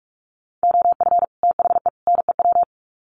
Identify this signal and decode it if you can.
Morse code, heard as a keyed tone, about 44 wpm, 700 Hz. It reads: OF THE NEW
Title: Sets of 3 Words 45wpm